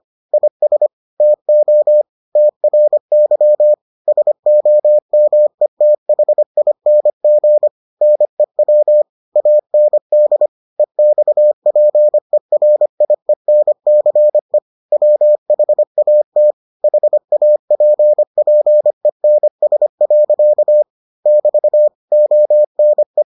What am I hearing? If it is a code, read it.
Morse code, heard as a keyed tone, about 25 wpm, 600 Hz. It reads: IS TO TRY SOMETHING NEW AND EXPERIENCE WHAT HAPPENS. <BT> ONE